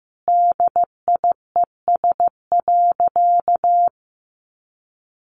Morse code, heard as a keyed tone, about 15 wpm, 700 Hz. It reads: DIES.
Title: Additional Easy Phrases 15wpm